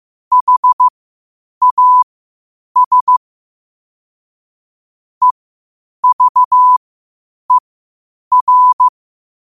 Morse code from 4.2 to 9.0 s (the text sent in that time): EVER